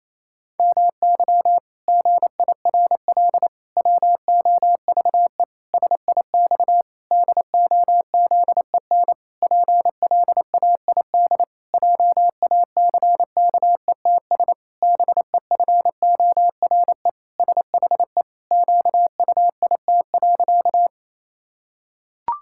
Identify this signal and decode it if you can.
Morse code, heard as a keyed tone, about 28 wpm, 700 Hz. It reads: MY GIRL WOVE SIX DOZEN PLAID JACKETS BEFORE SHE QUIT.
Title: Pangram Challenge v2 - 28wpm